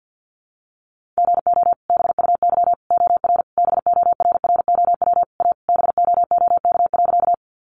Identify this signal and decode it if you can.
Morse code, heard as a keyed tone, about 48 wpm, 700 Hz. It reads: GO BUY OR BORROW A BOOK.
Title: Sentences from Top 1500 Words 50wpm